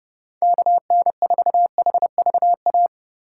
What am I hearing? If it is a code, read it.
Morse code, heard as a keyed tone, about 36 wpm, 700 Hz. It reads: KN4HVA
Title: Speed Racing - USA 2x3 Call Signs 30wpm